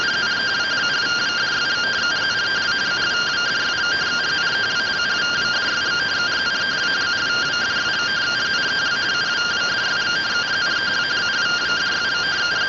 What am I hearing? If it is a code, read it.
Two-tone teleprinter signal, about 170 Hz shift.